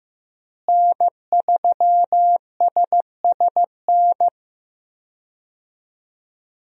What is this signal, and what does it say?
Morse code, heard as a keyed tone, about 15 wpm, 700 Hz. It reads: N3SSN
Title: USA Call Signs - 5 Characters - Repeated 3x - 15wpm